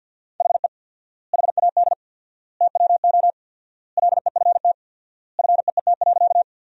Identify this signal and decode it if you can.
Morse code, heard as a keyed tone, about 47 wpm, 700 Hz. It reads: HE HAD TWO LEFT FEET.
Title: 4x Word-Spacing - Sentences from Top 300 Words - 50wpm